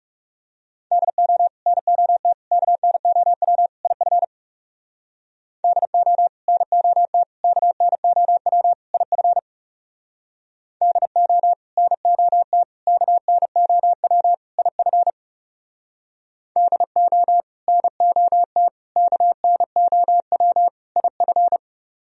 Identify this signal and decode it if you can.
Morse code, heard as a keyed tone, about 38 wpm, 700 Hz. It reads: DO NOT KNOW IF DO NOT KNOW IF DO NOT KNOW IF DO NOT KNOW IF